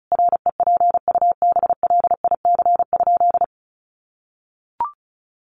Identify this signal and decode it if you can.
Morse code, heard as a keyed tone, about 34 wpm, 700 Hz. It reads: REPUBLIC?
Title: Sentences from Top 1500 Words 35wpm